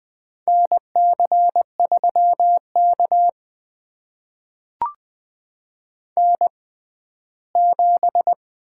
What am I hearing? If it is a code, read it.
Morse code, heard as a keyed tone, about 21 wpm, spaced as for 14 wpm, 700 Hz. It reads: NC3K N7
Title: USA Call Signs - 4 Characters - 6x Character Spacing - 20wpm